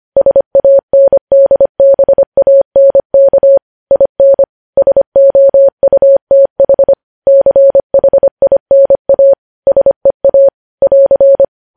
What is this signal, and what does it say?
Morse code, heard as a keyed tone, about 25 wpm, 550 Hz. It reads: SANDBANK IN SOUTH CHINA SEA <AR>
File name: podcast / News headlines-1x-25x25